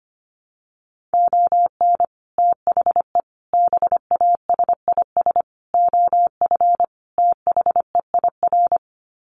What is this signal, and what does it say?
Morse code, heard as a keyed tone, about 25 wpm, 700 Hz. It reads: ON THE BASIS OF THEIR
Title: Sets of 5 Words Encore 25wpm